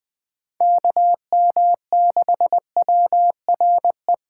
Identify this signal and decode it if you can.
Morse code, heard as a keyed tone, about 20 wpm, 700 Hz. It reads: KM6WRE
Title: Speed Racing - USA 2x3 Call Signs 17wpm